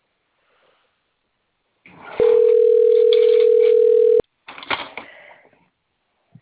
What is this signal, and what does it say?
Ringback tone (North America), 440 + 480 Hz.